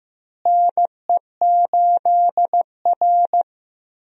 Morse code, heard as a keyed tone, about 15 wpm, 700 Hz. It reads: NE8R
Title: USA Call Signs - 4 Characters - Repeated 3x - 15wpm